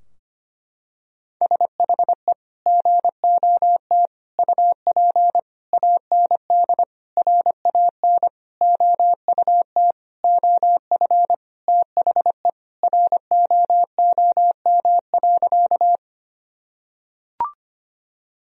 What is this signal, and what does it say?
Morse code, heard as a keyed tone, about 25 wpm, 700 Hz. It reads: SHE GOT UP AND RAN OUT OF THE ROOM.
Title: Sentences from Top 1000 Words - Review Part 3 - 25wpm